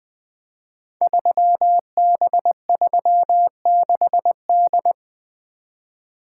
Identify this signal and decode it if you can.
Morse code, heard as a keyed tone, about 20 wpm, 700 Hz. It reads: 3B36D
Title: Code Groups - 5 Letter:Numbers - 20wpm